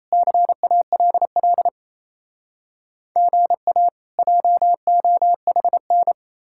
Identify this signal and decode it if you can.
Morse code, heard as a keyed tone, about 37 wpm, 700 Hz. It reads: CALL GA JOHN